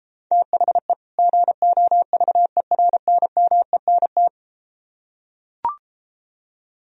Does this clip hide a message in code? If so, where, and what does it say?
Yes: Morse code from 0.3 to 4.3 s, heard as a keyed tone, about 34 wpm, 700 Hz. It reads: THE GOVERNMENT